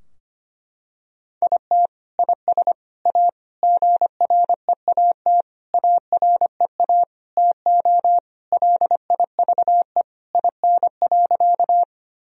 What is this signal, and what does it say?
Morse code, heard as a keyed tone, about 25 wpm, 700 Hz. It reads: IT IS A GREAT AREA TO LIVE IN.
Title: Sentences from Top 1000 Words - Review Part 2 - 25wpm